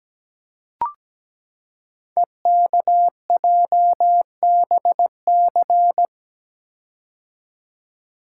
Morse code, heard as a keyed tone, about 17 wpm, 700 Hz. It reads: EKJBC
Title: Code Groups - 5 Letters - 17wpm